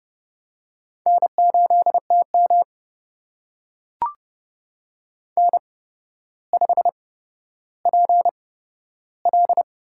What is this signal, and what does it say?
Morse code, heard as a keyed tone, about 31 wpm, spaced as for 13 wpm, 700 Hz. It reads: N8TM N5PL
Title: USA Call Signs - 8x Character Spacing - 30wpm